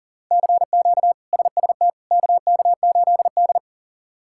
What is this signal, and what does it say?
Morse code, heard as a keyed tone, about 37 wpm, 700 Hz. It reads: CQ SST KK8D
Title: K1USN SST QSO Practice - 40wpm